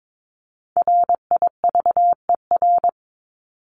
Morse code, heard as a keyed tone, about 22 wpm, 700 Hz. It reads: RIVER
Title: Top 500 Words 22wpm